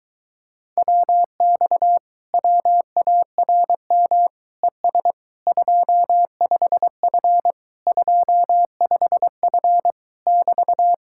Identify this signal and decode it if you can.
Morse code, heard as a keyed tone, about 23 wpm, 700 Hz. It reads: WX WARM ES 25F 25F <BT>